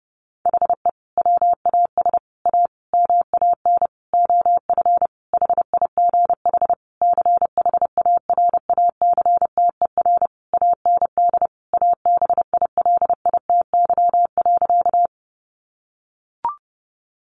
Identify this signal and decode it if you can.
Morse code, heard as a keyed tone, about 30 wpm, 700 Hz. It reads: HE WAS A MAN OF HIGH CHARACTER AND ABILITY.
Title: Sentences from Top 1200 Words 30wpm